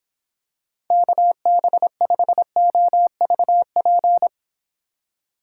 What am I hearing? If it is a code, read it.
Morse code, heard as a keyed tone, about 25 wpm, 700 Hz. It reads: KB5OVP